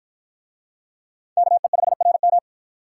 Morse code, heard as a keyed tone, about 48 wpm, 700 Hz. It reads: KE5RM